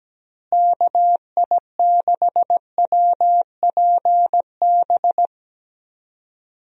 Morse code, heard as a keyed tone, about 17 wpm, 700 Hz. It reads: KI6WPB
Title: USA Call Signs - 6 Characters - Repeated 3x - 17wpm